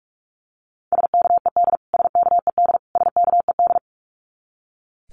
Morse code, heard as a keyed tone, about 44 wpm, 700 Hz. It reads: SKED SKED SKED
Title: Abbreviations Repeated 3x - 45wpm